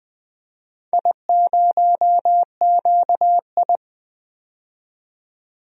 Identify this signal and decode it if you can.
Morse code, heard as a keyed tone, about 20 wpm, 700 Hz. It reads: I0QI
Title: DXCC-100 Call Signs - 4 Characters - 20wpm